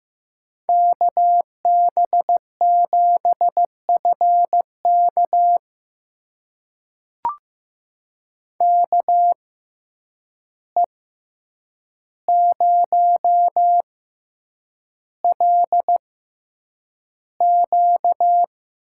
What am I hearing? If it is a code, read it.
Morse code, heard as a keyed tone, about 15 wpm, spaced as for 8 wpm, 700 Hz. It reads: KB7FK KE0LQ
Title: USA Call Signs - 5 Characters - 6x Character Spacing - 15wpm